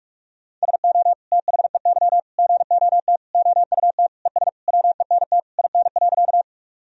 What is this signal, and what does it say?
Morse code, heard as a keyed tone, about 43 wpm, 700 Hz. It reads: SO THEY GOT OUT ES WENT IN.
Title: Sentences from Top 300 Words and Abbreviations 45wpm